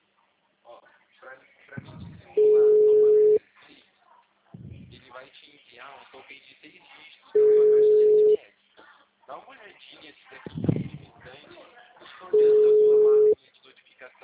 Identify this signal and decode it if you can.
Ringback tone (Europe, 425 Hz), with speech.